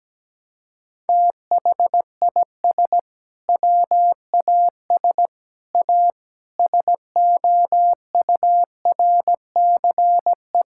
Morse code, heard as a keyed tone, about 17 wpm, 700 Hz. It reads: THIS WAS A SOURCE
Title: Sentences from Top 1000 Words - Review Part 5 - 17wpm